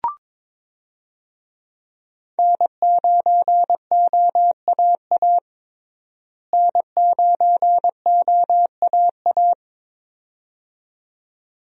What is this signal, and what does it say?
Morse code, heard as a keyed tone, about 22 wpm, 700 Hz. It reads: N9OAA N9OAA
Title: USA Call Signs - 5 Characters - Repeated 2x - 22wpm